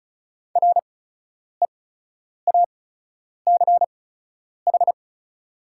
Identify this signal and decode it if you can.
Morse code, heard as a keyed tone, about 36 wpm, spaced as for 10 wpm, 690 Hz. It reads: REACH
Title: 8x Character Spacing - Top 500 Words - 35wpm